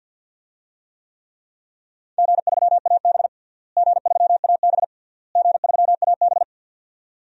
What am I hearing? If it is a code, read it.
Morse code, heard as a keyed tone, about 52 wpm, 700 Hz. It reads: G3AB G3AB G3AB
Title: DXCC-100 Call Signs - 4 Characters - Repeated 3x - 50wpm